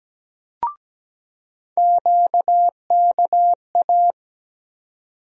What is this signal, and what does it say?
Morse code, heard as a keyed tone, about 17 wpm, 700 Hz. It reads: QKA
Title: USA Call Sign Postfixes - 3 Letters 17wpm